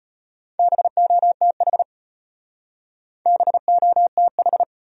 Morse code, heard as a keyed tone, about 32 wpm, 700 Hz. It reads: BOTH BOTH